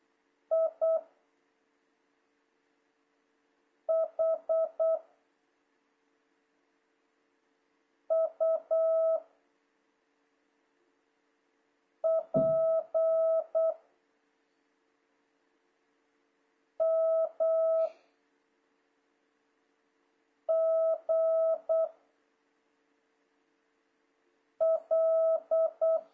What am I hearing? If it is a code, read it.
Morse code, heard as a keyed tone, about 8 wpm, 660 Hz. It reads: I H U P M G L
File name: jh8cbh / 2026cw